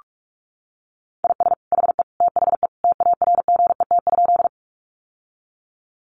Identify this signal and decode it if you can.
Morse code, heard as a keyed tone, about 46 wpm, 700 Hz. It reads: IS HE THE TARGET?
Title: Sentences from Top 1200 Words 45wpm